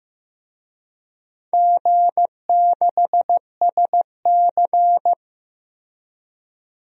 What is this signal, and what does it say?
Morse code, heard as a keyed tone, about 15 wpm, 700 Hz. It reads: G6SC
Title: DXCC-100 Call Signs - 4 Characters - Repeated 3x - 15wpm